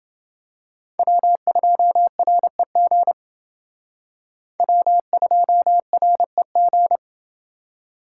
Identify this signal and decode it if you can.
Morse code, heard as a keyed tone, about 29 wpm, 700 Hz. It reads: W2REG W2REG